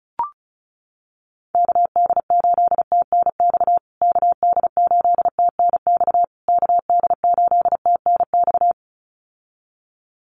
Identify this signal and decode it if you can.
Morse code, heard as a keyed tone, about 34 wpm, 700 Hz. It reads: KD8TNX KD8TNX KD8TNX
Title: USA Call Signs Repeated 3x 35wpm